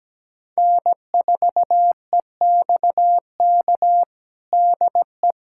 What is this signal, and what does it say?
Morse code, heard as a keyed tone, about 17 wpm, 700 Hz. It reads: N4EXK DE
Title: USA Callsign DE Callsign - 5 Characters - 17wpm